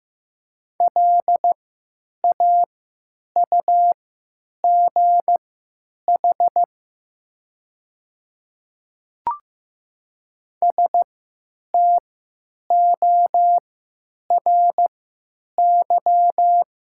Morse code, heard as a keyed tone, about 15 wpm, spaced as for 9 wpm, 700 Hz. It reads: LAUGH STORY
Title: 3x Character Spacing - Top 500 Words - 15wpm